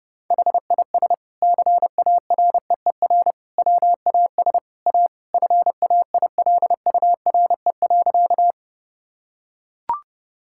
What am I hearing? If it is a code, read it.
Morse code, heard as a keyed tone, about 30 wpm, 700 Hz. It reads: HIS CAREER WAS A FAILURE.
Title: Sentences from Top 1000 Words - Review Part 6 - 30wpm